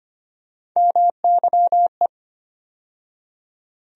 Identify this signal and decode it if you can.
Morse code, heard as a keyed tone, about 25 wpm, 700 Hz. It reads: MYE
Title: USA Call Sign Postfixes - 3 Letters 25wpm